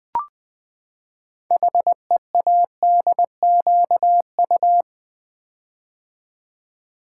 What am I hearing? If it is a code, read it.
Morse code, heard as a keyed tone, about 20 wpm, 700 Hz. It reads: HEADQU
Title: Build A Word - 6 Letters At A Time - 12-Letter Words - 20wpm